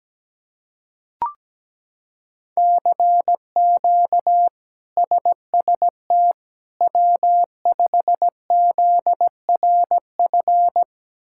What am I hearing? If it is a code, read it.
Morse code, heard as a keyed tone, about 17 wpm, 700 Hz. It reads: CQ SST W5ZRF
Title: K1USN SST QSO Practice - 17wpm